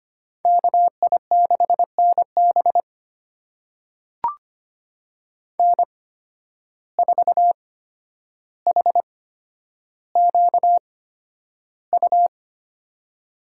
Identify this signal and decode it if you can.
Morse code, heard as a keyed tone, about 25 wpm, spaced as for 11 wpm, 700 Hz. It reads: KI6NB N4HQU
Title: USA Call Signs - 5 Characters - 8x Character Spacing - 25wpm